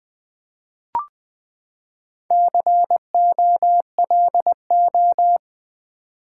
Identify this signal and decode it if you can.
Morse code, heard as a keyed tone, about 20 wpm, 700 Hz. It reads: COLO
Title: Build A Word - 4 Letters At A Time - 8-Letter Words - 20wpm